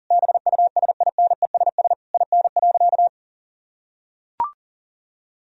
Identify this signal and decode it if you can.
Morse code, heard as a keyed tone, about 40 wpm, 700 Hz. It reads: BUSINESS IN.
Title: Sentences from Top 1000 Words - Review Part 3 - 40wpm